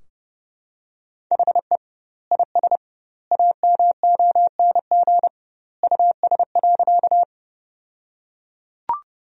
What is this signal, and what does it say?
Morse code, heard as a keyed tone, about 30 wpm, 700 Hz. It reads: HE IS AMONG US.
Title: 2x Word-Spacing - Sentences from Top 500 Words - 30wpm